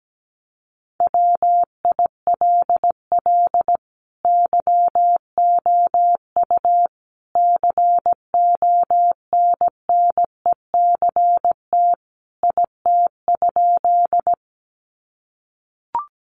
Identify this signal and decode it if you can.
Morse code, heard as a keyed tone, about 17 wpm, 700 Hz. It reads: WILL YOU CONNECT IT?
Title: Sentences from Top 1500 Words 17wpm